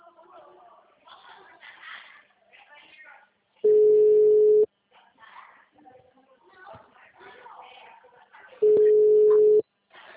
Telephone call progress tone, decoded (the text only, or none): ringback tone